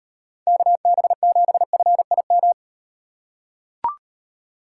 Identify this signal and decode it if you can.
Morse code, heard as a keyed tone, about 38 wpm, 700 Hz. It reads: KB7FIM